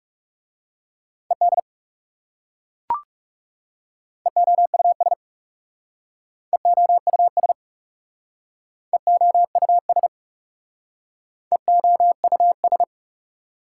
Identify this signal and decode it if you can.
Morse code, heard as a keyed tone, about 37 wpm, 700 Hz. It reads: ED EOUS EOUS EOUS EOUS
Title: Speed-Racing Prefixes and Suffixes 30wpm